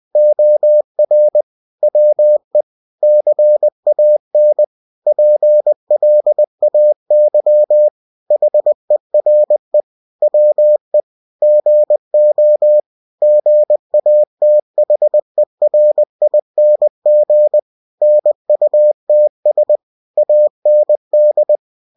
Morse code, heard as a keyed tone, about 20 wpm, 600 Hz. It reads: OR WE CAN PLAY HERE WE GO GATHERING NUTS AND
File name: cw-books / winnie-the-pooh / winnie-20wpm-page-0080